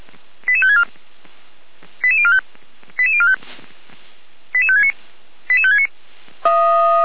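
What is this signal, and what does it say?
Five-tone selective call (ZVEI), heard as quick runs of five tones, tones of about 70 ms. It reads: 88564 88045 88045 88569 88569
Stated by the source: ZVEI five-tone selective call received on SDR.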